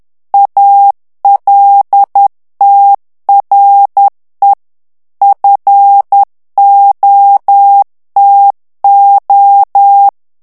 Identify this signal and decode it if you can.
Morse code, heard as a keyed tone, about 10 wpm, 800 Hz. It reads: ALTRE FOTO
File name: morse mp3 / ITALIEN texte clair 1 vitesse 10 mots minute